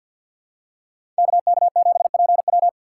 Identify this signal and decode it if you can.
Morse code, heard as a keyed tone, about 46 wpm, 700 Hz. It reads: KK7PW
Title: USA Call Signs - 5 Characters - 2x Character Spacing - 50wpm